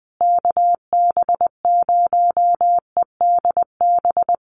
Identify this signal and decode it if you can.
Morse code, heard as a keyed tone, about 20 wpm, 700 Hz. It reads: KB0EDB
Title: USA Call Signs Repeated 3x 20wpm